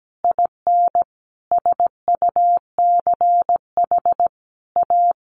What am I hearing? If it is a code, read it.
Morse code, heard as a keyed tone, about 17 wpm, 700 Hz. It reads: IN SUCH A
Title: Sentences from Top 1500 Words 17wpm